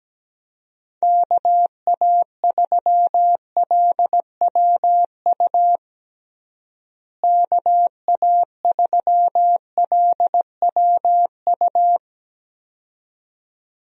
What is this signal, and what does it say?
Morse code, heard as a keyed tone, about 17 wpm, 700 Hz. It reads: KA3LWU KA3LWU
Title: USA Call Signs - 6 Characters - Repeated 2x - 17wpm